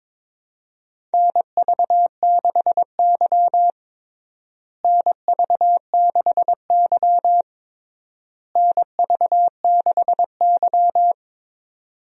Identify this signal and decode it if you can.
Morse code, heard as a keyed tone, about 22 wpm, 700 Hz. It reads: NV6Y NV6Y NV6Y
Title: USA Call Signs - 4 Characters - Repeated 3x - 22wpm